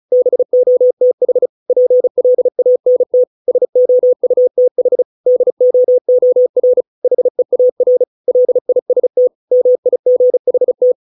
Morse code, heard as a keyed tone, about 34 wpm, 500 Hz. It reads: BOTH PLANT SOUTH DOOR HEAR LIST MIGHT